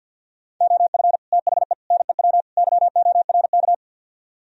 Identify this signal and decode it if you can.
Morse code, heard as a keyed tone, about 62 wpm, 700 Hz. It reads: OF THE NEW YORK
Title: Speed-Racing - Sets of 4 Words 50wpm